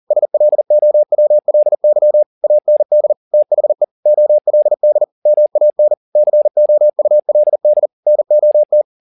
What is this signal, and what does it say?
Morse code, heard as a keyed tone, about 40 wpm, 600 Hz. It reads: SLOWLY AND THE OLD MAN COULD NOT